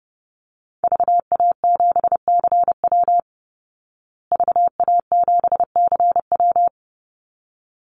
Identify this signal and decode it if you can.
Morse code, heard as a keyed tone, about 30 wpm, 700 Hz. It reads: VA7CW VA7CW
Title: Call Signs Repeated 3x - 30wpm